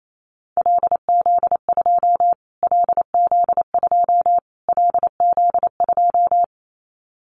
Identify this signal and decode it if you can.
Morse code, heard as a keyed tone, about 27 wpm, 700 Hz. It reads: LZ2 LZ2 LZ2
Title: Callsign Prefixes for Top 25 DXCC - Repeated 3x - 28wpm